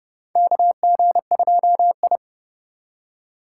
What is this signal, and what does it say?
Morse code, heard as a keyed tone, about 30 wpm, 700 Hz. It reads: KG2I
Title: USA Call Signs - 4 Characters - 2x Character Spacing - 30wpm